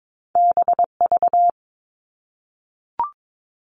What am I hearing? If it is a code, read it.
Morse code, heard as a keyed tone, about 22 wpm, 700 Hz. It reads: BV
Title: All Cut Numbers - 2-Digit 22wpm